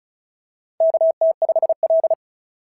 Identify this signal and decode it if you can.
Morse code, heard as a keyed tone, about 36 wpm, 690 Hz. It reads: KT5L
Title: SOTA QSO Practice 35wpm